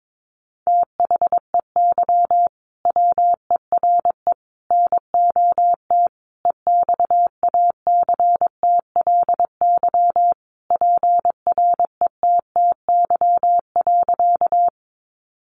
Morse code, heard as a keyed tone, about 22 wpm, 700 Hz. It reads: THEY WERE NOT EXACTLY PRETTY.
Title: Sentences from Top 1000 Words 22wpm